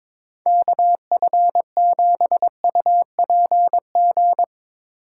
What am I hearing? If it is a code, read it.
Morse code, heard as a keyed tone, about 23 wpm, 700 Hz. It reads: KF7UPG
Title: Speed Racing - USA 2x3 Call Signs 22wpm